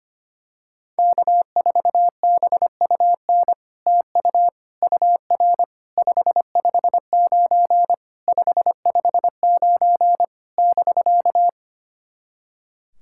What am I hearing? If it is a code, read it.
Morse code, heard as a keyed tone, about 25 wpm, 700 Hz. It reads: K4BUN TU UR 559 559 <BK>
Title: SOTA QSO Practice 25wpm